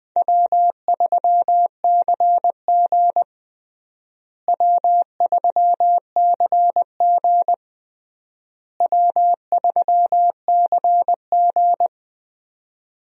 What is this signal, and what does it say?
Morse code, heard as a keyed tone, about 20 wpm, 700 Hz. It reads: W3CG W3CG W3CG
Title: USA Call Signs - 4 Characters - Repeated 3x - 20wpm